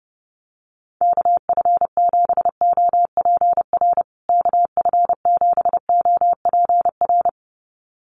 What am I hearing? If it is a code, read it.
Morse code, heard as a keyed tone, about 30 wpm, 700 Hz. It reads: KF7OPR KF7OPR
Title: USA Call Signs Repeated 2x 30wpm